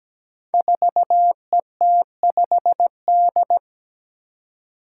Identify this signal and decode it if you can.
Morse code, heard as a keyed tone, about 17 wpm, 700 Hz. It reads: 4ET5D
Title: Five Challenging Letter-Number Combinations 17wpm